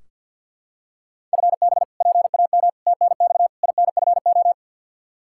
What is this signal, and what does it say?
Morse code, heard as a keyed tone, about 46 wpm, 700 Hz. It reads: FB PAM TNX INFO
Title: K1USN SST QSO Practice 50wpm